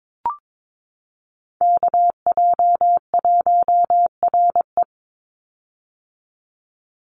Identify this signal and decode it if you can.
Morse code, heard as a keyed tone, about 22 wpm, 700 Hz. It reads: KJ1RE
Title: Call Signs - 22wpm